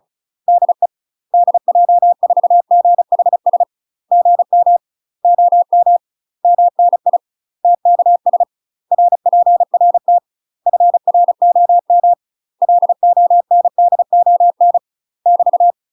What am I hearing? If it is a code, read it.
Morse code, heard as a keyed tone, about 35 wpm, 700 Hz. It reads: DE DJ4GHS GM OM MNI TKS RPRT FROM LONDON <BT>
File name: UK MASTER QSOs TRAINING / 35WPM / UK MASTER QSOs 0010